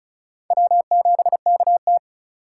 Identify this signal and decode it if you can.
Morse code, heard as a keyed tone, about 46 wpm, 700 Hz. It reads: W7KT